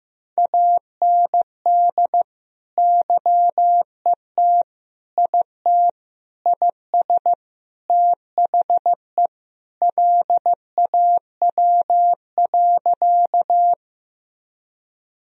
Morse code, heard as a keyed tone, about 14 wpm, 700 Hz. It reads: AND YET IT IS THE LAW.
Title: Sentences from Top 1000 Words - Review Part 3 - 15wpm